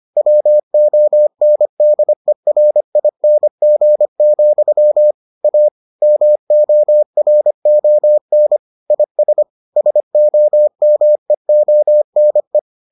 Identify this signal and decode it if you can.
Morse code, heard as a keyed tone, about 25 wpm, 600 Hz. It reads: WONDERING, A MORON IS SOMEONE